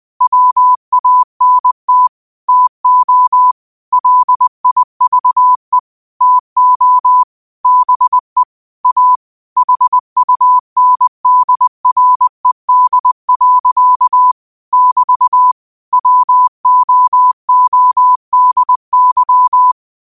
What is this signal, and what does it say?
Morse code, heard as a keyed tone, about 20 wpm, 1000 Hz. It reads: WANT TO LIVE TO BE A HUNDRED. <BT> WOODY
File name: qotdfiles / 20wpm